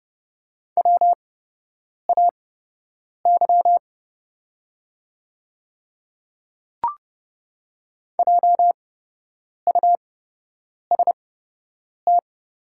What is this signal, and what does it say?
Morse code, heard as a keyed tone, about 30 wpm, spaced as for 8 wpm, 700 Hz. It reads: WAY JUST
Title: 8x Character Spacing - Top 200 Words - 30wpm